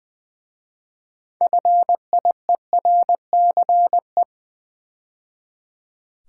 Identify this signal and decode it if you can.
Morse code, heard as a keyed tone, about 20 wpm, 700 Hz. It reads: FIERCE
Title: 6 Letter Words - First 3 Letters Given - 20wpm